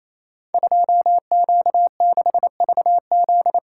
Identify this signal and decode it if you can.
Morse code, heard as a keyed tone, about 28 wpm, 690 Hz. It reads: 2Q6VZ
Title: Code Groups - 5 Letter:Numbers - 28wpm